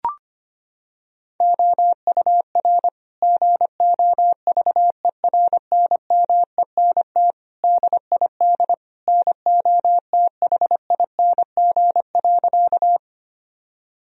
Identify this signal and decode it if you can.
Morse code, heard as a keyed tone, about 25 wpm, 700 Hz. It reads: OUR GOVERNMENT DID NOTHING.
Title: Sentences from Top 1000 Words - Review Part 3 - 25wpm